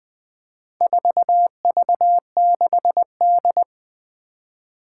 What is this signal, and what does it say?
Morse code, heard as a keyed tone, about 20 wpm, 700 Hz. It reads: 4V6D
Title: Four Challenging Letter-Number Combinations 20wpm